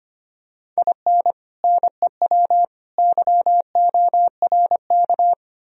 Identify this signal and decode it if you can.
Morse code, heard as a keyed tone, about 28 wpm, 700 Hz. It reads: IN NEW YORK
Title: Speed-Racing - Sets of 3 Words 25wpm